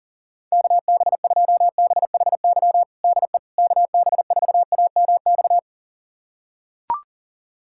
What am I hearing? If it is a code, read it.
Morse code, heard as a keyed tone, about 40 wpm, 700 Hz. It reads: KB2BHY DE KB4AMX
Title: USA Callsign DE Callsign - 6 Characters - 40wpm